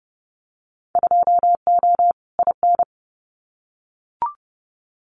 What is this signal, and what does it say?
Morse code, heard as a keyed tone, about 30 wpm, 700 Hz. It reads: 2O IN
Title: Winter Field Day Exchanges 30wpm